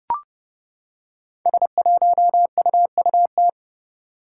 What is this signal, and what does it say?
Morse code, heard as a keyed tone, about 30 wpm, 700 Hz. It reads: S1UUT
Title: Five Challenging Letter-Number Combinations 30wpm